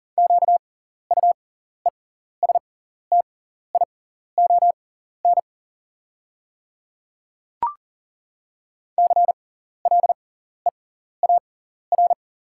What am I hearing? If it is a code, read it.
Morse code, heard as a keyed tone, about 40 wpm, spaced as for 14 wpm, 690 Hz. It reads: QUESTION CLEAR
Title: 6x Character Spacing - Top 500 Words - 40wpm